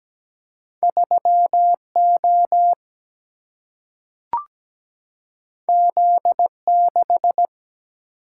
Morse code, heard as a keyed tone, about 17 wpm, 700 Hz. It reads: 3O Z6
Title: Single-Letter Single-Number 17wpm